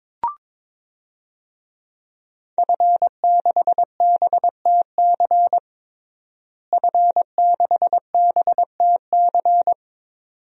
Morse code, heard as a keyed tone, about 21 wpm, 700 Hz. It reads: F6BTC F6BTC
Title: DXCC-100 Call Signs - 5 Characters - Repeated 2x - 22wpm